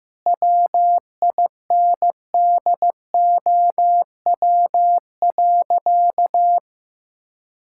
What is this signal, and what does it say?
Morse code, heard as a keyed tone, about 15 wpm, 700 Hz. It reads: WINDOW.
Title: Sentences from Top 1600 Words 15wpm